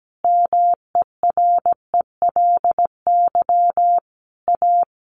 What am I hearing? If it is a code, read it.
Morse code, heard as a keyed tone, about 17 wpm, 700 Hz. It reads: MERELY A
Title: Sentences from Top 1400 Words 17wpm